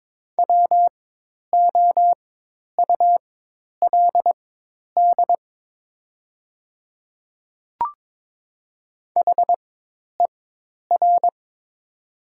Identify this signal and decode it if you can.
Morse code, heard as a keyed tone, about 22 wpm, spaced as for 10 wpm, 700 Hz. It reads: WOULD HER
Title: 4x Character Spacing - Top 200 Words - 22wpm